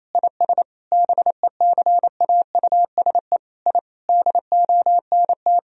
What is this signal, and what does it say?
Morse code, heard as a keyed tone, about 28 wpm, 700 Hz. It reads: IS BECAUSE I DONT
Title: Sentences from Top 1000 Words - Review Part 3 - 28wpm